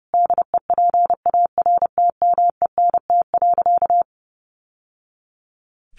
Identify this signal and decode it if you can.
Morse code, heard as a keyed tone, about 30 wpm, 700 Hz. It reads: DEPARTMENT.
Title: Sentences from Top 1200 Words 30wpm